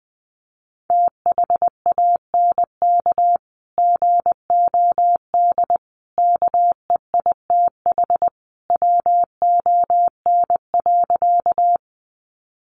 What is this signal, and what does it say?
Morse code, heard as a keyed tone, about 20 wpm, 700 Hz. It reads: THANK GOD KEITH WON.
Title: A Sound Of Thunder 20wpm